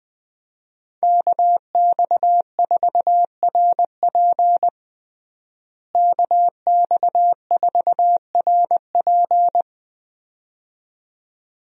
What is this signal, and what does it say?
Morse code, heard as a keyed tone, about 21 wpm, 700 Hz. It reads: KX4RP KX4RP
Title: USA Call Signs - 5 Characters - Repeated 2x - 20wpm